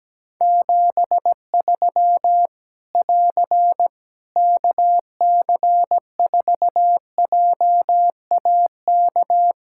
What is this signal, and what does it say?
Morse code, heard as a keyed tone, about 17 wpm, 700 Hz. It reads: 73 <AR> KC4JAK
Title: Practice QSO - Exchange 4 - Ending - 17wpm